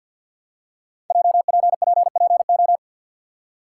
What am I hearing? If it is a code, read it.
Morse code, heard as a keyed tone, about 53 wpm, 700 Hz. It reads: JPPPO
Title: Code Groups - 5 Letters - 50wpm